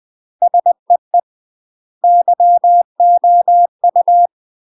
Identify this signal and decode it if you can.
Morse code, heard as a keyed tone, about 20 wpm, 700 Hz. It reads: SEE YOU
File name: UK MASTER QSOs TRAINING/20WPM